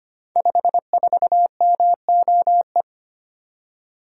Morse code, heard as a keyed tone, about 25 wpm, 700 Hz. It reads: 54MOE
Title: Code Groups - 5 Letter:Numbers - 25wpm